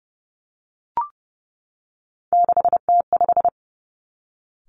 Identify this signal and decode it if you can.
Morse code, heard as a keyed tone, about 30 wpm, 700 Hz. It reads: 6T5
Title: Three Challenging Letter-Number Combinations 30wpm